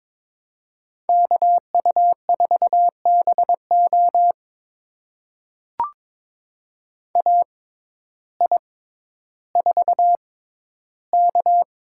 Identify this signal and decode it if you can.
Morse code, heard as a keyed tone, about 22 wpm, spaced as for 12 wpm, 700 Hz. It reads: KU4BO AI4K
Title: USA Call Signs - 5 Characters - 6x Character Spacing - 22wpm